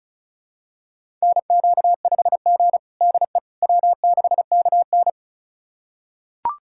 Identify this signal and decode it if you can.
Morse code, heard as a keyed tone, about 36 wpm, 700 Hz. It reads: NQ5G DE W6KN
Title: USA Callsign DE Callsign - 4 Characters - 35wpm